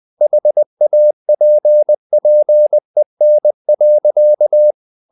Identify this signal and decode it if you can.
Morse code, heard as a keyed tone, about 20 wpm, 600 Hz. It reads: HAPPEN.
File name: the old man and the sea / the old man and the sea part 1 20wpm